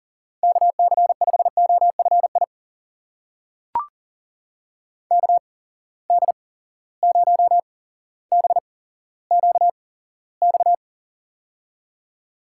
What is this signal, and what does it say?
Morse code, heard as a keyed tone, about 40 wpm, spaced as for 17 wpm, 700 Hz. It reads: KC5OFI KD0BQX
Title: USA Call Signs - 6 Characters - 8x Character Spacing - 40wpm